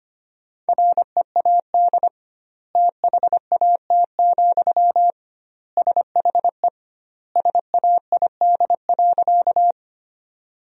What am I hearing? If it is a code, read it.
Morse code, heard as a keyed tone, about 25 wpm, 700 Hz. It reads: READ THAT, SHE SAID.
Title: 2x Word-Spacing - Sentences from Top 200 Words - 25wpm